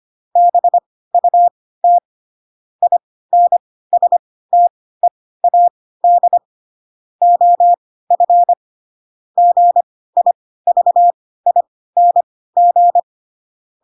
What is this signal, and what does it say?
Morse code, heard as a keyed tone, about 25 wpm, spaced as for 16 wpm, 700 Hz. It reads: BUT INSTEAD OF GIVING
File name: Pinocchio / Pinocchio ch3 10 25 ik1hge